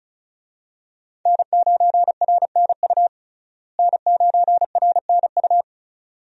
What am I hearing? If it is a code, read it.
Morse code, heard as a keyed tone, about 34 wpm, 700 Hz. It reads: N9RNU N9RNU
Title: USA Call Signs - 5 Characters - Repeated 2x - 35wpm